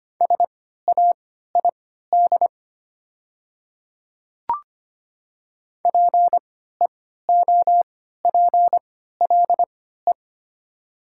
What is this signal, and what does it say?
Morse code, heard as a keyed tone, about 25 wpm, spaced as for 14 wpm, 700 Hz. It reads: SAID PEOPLE
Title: 3x Character Spacing - Top 100 Words - 25wpm